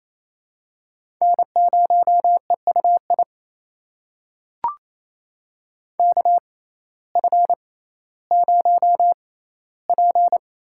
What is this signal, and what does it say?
Morse code, heard as a keyed tone, about 28 wpm, spaced as for 15 wpm, 700 Hz. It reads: N0EUI KF0P
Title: USA Call Signs - 5 Characters - 6x Character Spacing - 28wpm